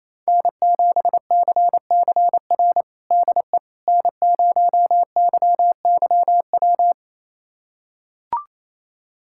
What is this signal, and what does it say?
Morse code, heard as a keyed tone, about 28 wpm, 700 Hz. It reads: N7CCR DE N0YYW
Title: USA Callsign DE Callsign - 5 Characters - 28wpm